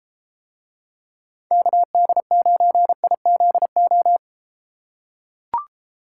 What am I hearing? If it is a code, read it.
Morse code, heard as a keyed tone, about 32 wpm, 700 Hz. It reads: KD9IZO